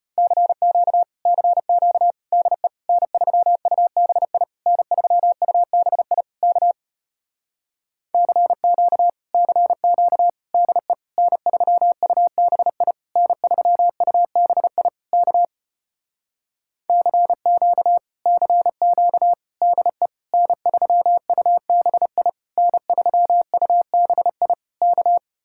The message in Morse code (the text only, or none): CQ CQ DE N3UBI N3UBI K CQ CQ DE N3UBI N3UBI K CQ CQ DE N3UBI N3UBI K